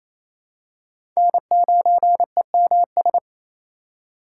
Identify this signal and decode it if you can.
Morse code, heard as a keyed tone, about 28 wpm, 700 Hz. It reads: N9EMS
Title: USA Call Signs - 5 Characters - 8x Character Spacing - 28wpm